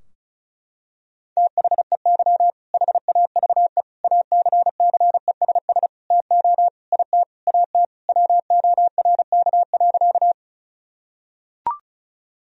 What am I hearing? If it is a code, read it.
Morse code, heard as a keyed tone, about 34 wpm, 700 Hz. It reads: THEY HAVE ACCESS TO IT AT WORK.
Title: Sentences from Top 1000 Words - Review Part 4 - 35wpm